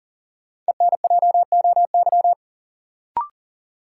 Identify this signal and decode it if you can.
Morse code, heard as a keyed tone, about 40 wpm, 700 Hz. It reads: ENJOY
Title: Sentences from Top 1000 Words - Review Part 4 - New Word Drill - 40wpm